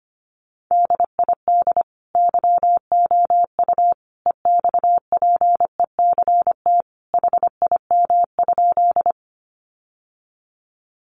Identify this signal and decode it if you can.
Morse code, heard as a keyed tone, about 25 wpm, 700 Hz. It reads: DID YOU EXPECT HIM?
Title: Sentences from Top 1100 Words 25wpm